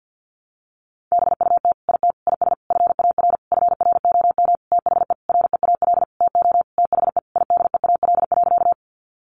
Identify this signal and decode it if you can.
Morse code, heard as a keyed tone, about 50 wpm, 700 Hz. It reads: BUT IT IS FAR FROM THE REAL TO THE IDEAL.
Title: Sentences from Top 1400 Words 50wpm